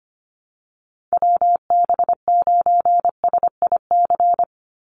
Morse code, heard as a keyed tone, about 25 wpm, 700 Hz. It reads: WB9SIC
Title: USA Call Signs 25wpm